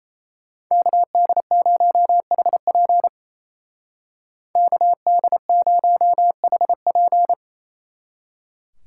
Morse code, heard as a keyed tone, about 34 wpm, 700 Hz. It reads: KD0HP KD0HP
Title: Speed Racing - USA Call Signs 28wpm